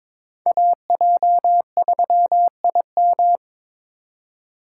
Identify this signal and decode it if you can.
Morse code, heard as a keyed tone, about 22 wpm, 700 Hz. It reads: AJ3IM
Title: USA Call Signs - 5 Characters - 4x Character Spacing - 22wpm